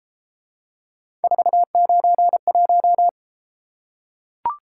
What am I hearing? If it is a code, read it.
Morse code, heard as a keyed tone, about 29 wpm, 700 Hz. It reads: 491